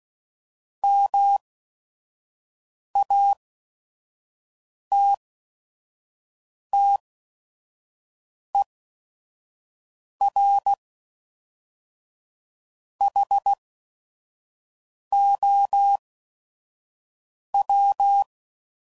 Morse code, heard as a keyed tone, about 16 wpm, spaced as for 5 wpm, 800 Hz. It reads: MATTER HOW
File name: CW Code Groups @5WPM / Technical Subjects / AND Gate